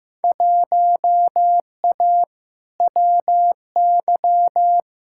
Morse code, heard as a keyed tone, about 15 wpm, 700 Hz. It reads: 1A WY
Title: Field Day Exchanges - 15wpm